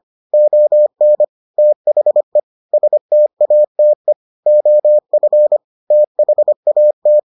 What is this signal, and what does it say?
Morse code, heard as a keyed tone, about 25 wpm, 600 Hz. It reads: ON THE STATE OF THAT